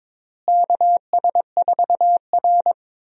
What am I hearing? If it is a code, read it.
Morse code, heard as a keyed tone, about 22 wpm, 700 Hz. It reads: KS4R
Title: USA Call Signs - 4 Characters - Repeated 3x - 22wpm